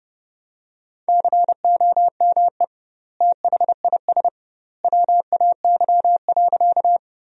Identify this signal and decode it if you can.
Morse code, heard as a keyed tone, about 32 wpm, 700 Hz. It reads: COME THIS WAY.
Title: 2x Word-Spacing - Sentences from Top 100 Words - 30wpm